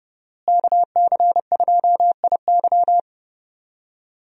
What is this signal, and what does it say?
Morse code, heard as a keyed tone, about 32 wpm, 700 Hz. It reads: KC2IY
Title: USA Call Signs - 5 Characters - Repeated 3x - 30wpm